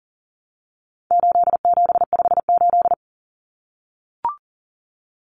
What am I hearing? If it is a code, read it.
Morse code, heard as a keyed tone, about 40 wpm, 700 Hz. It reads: 8758
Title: Numbers - 4-Digit 40wpm